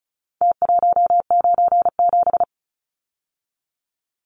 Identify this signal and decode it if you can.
Morse code, heard as a keyed tone, about 34 wpm, 700 Hz. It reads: T197
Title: Numbers - Contest Style - 4-Digit 35wpm